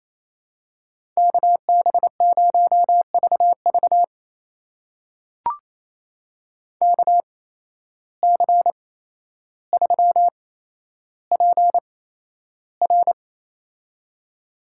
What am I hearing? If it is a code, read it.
Morse code, heard as a keyed tone, about 27 wpm, spaced as for 12 wpm, 700 Hz. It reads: KB0VV KC3PR
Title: USA Call Signs - 5 Characters - 8x Character Spacing - 28wpm